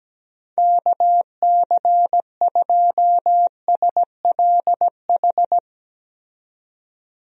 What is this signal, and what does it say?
Morse code, heard as a keyed tone, about 17 wpm, 700 Hz. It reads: KC2SLH
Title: USA Call Signs - 6 Characters - 17wpm